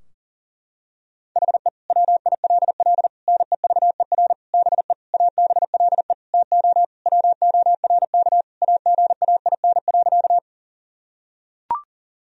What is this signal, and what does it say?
Morse code, heard as a keyed tone, about 40 wpm, 700 Hz. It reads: HE WILL NEVER BE ABLE TO WORK AGAIN.
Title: Sentences from Top 1000 Words - Review Part 2 - 40wpm